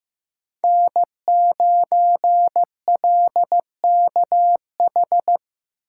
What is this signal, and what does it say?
Morse code, heard as a keyed tone, about 15 wpm, 700 Hz. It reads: N9LKH
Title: USA Call Signs - 5 Characters - 2x Character Spacing - 15wpm